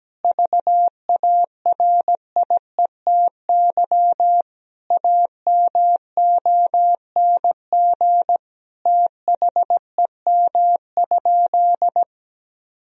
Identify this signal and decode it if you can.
Morse code, heard as a keyed tone, about 17 wpm, 700 Hz. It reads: VARIETY AMONG THEM?
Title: Sentences from Top 1000 Words - Review Part 4 - 17wpm